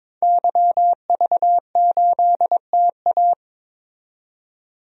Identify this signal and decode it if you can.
Morse code, heard as a keyed tone, about 21 wpm, 700 Hz. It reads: YV8TA
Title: DXCC-100 Call Signs - 5 Characters - 22wpm